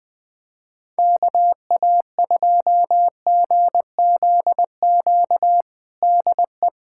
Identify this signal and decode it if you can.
Morse code, heard as a keyed tone, about 20 wpm, 700 Hz. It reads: KA2GZQ DE
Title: USA Callsign DE Callsign 20wpm